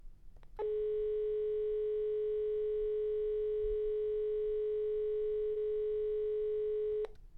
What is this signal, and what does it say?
Dial tone (Europe, 425 Hz).